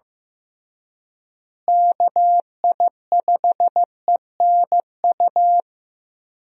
Morse code, heard as a keyed tone, about 16 wpm, 700 Hz. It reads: KI5ENU
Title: USA Call Signs - 6 Characters - Repeated 2x - 15wpm